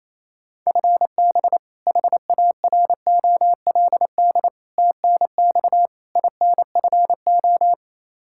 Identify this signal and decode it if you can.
Morse code, heard as a keyed tone, about 27 wpm, 700 Hz. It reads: FB HAROLD TNX INFO
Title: Practice QSO - Exchange 3 - More Info - 28wpm